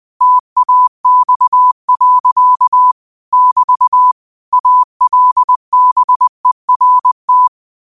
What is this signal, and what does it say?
Morse code, heard as a keyed tone, about 20 wpm, 1000 Hz. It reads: TAX. <BT> ALBERT
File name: qotdfiles / 20wpm